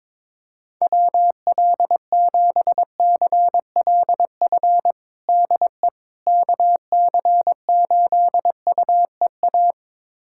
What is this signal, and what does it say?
Morse code, heard as a keyed tone, about 22 wpm, 700 Hz. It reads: WL7CLF DE KC8UEA
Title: USA Callsign DE Callsign - 6 Characters - 22wpm